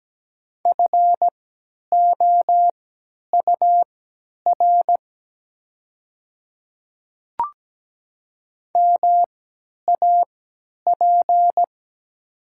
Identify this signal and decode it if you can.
Morse code, heard as a keyed tone, about 17 wpm, spaced as for 10 wpm, 700 Hz. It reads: FOUR MAP
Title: 3x Character Spacing - Top 500 Words - 17wpm